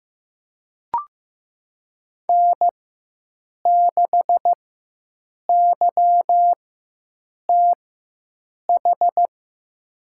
Morse code, heard as a keyed tone, about 15 wpm, spaced as for 7 wpm, 700 Hz. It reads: N6YTH
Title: USA Call Signs - 5 Characters - 4x Character Spacing - 15wpm